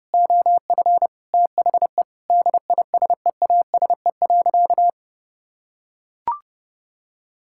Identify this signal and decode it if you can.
Morse code, heard as a keyed tone, about 28 wpm, 700 Hz. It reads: OF THE DISEASE.
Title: Sentences from Top 1000 Words - Review Part 6 - 30wpm